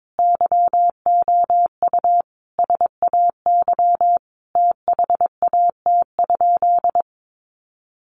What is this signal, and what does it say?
Morse code, heard as a keyed tone, about 22 wpm, 700 Hz. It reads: YOU SAY THAT?
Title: Sentences from Top 200 Words 22wpm